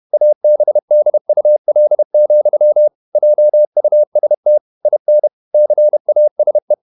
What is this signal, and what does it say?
Morse code, heard as a keyed tone, about 31 wpm, 600 Hz. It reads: ABDUL, JUST IN CASE